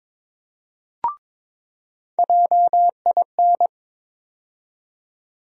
Morse code, heard as a keyed tone, about 22 wpm, 700 Hz. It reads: JIN
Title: USA Call Sign Postfixes - 3 Letters 22wpm